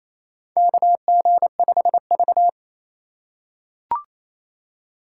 Morse code, heard as a keyed tone, about 28 wpm, 700 Hz. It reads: KG5V
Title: K1USN SST QSO Practice - 28wpm